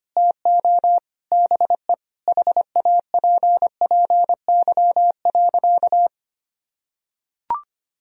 Morse code, heard as a keyed tone, about 25 wpm, 700 Hz. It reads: TO BE HAPPY.
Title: Sentences from Top 1000 Words - Review Part 1 - 25wpm